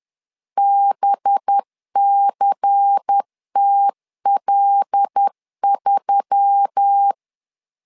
Morse code, heard as a keyed tone, about 10 wpm, 800 Hz. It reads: BCTL3
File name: LEZIONI / 5 LBVC / 5 LBVC 10wpm 2349